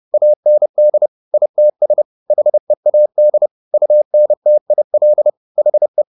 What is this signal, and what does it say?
Morse code, heard as a keyed tone, about 30 wpm, 600 Hz. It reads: AND ITS HEAD UNTIL HE